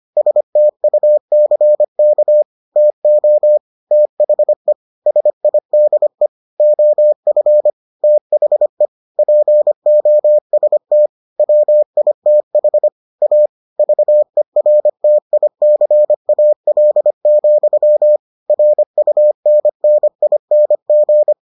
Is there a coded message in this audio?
Yes: Morse code from 0.2 to 21.3 s, heard as a keyed tone, about 25 wpm, 600 Hz. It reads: STUCK TO THE SIDE OF THE POST WITH A VERTICAL, RUNNING